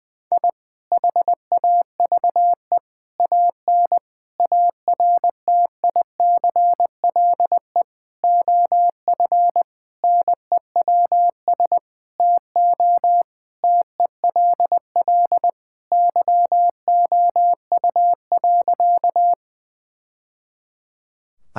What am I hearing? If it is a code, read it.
Morse code, heard as a keyed tone, about 20 wpm, 700 Hz. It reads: I HAVE AN ARTICLE OF NEWS TO TELL YOU.
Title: Sentences from Top 1000 Words - Review Part 5 - 20wpm